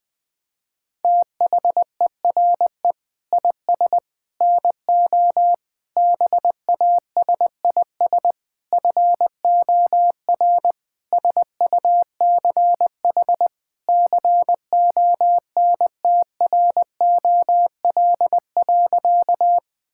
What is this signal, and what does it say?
Morse code, heard as a keyed tone, about 20 wpm, 700 Hz. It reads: THERE IS NO BASIS FOR SUCH CONTROL.
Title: Sentences from Top 1000 Words - Review Part 5 - 20wpm